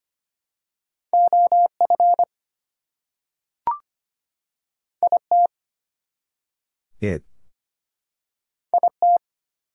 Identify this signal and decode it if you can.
Morse code, heard as a keyed tone, about 25 wpm, 700 Hz. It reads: OF IT IT
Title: Two Letter Words - 25wpm